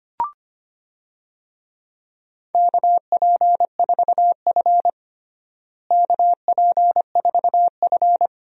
Morse code, heard as a keyed tone, about 25 wpm, 700 Hz. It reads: KP4F KP4F
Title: DXCC-100 Call Signs - 4 Characters - Repeated 2x - 25wpm